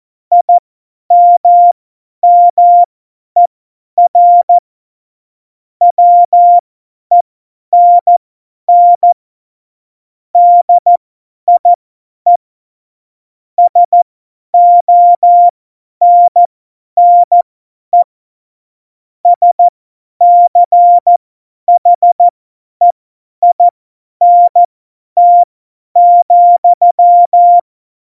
Morse code, heard as a keyed tone, about 14 wpm, spaced as for 10 wpm, 700 Hz. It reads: IMMER WENN DIE SONNE SCHEINT,
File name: Morsekurs / Witze / Witz 05. September 17, Tempo 50